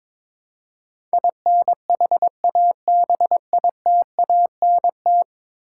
Morse code, heard as a keyed tone, about 21 wpm, 700 Hz. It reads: INHABITANT
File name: practice / 22wpm / 10 Letter Words - First 3 Letters Given - 22wpm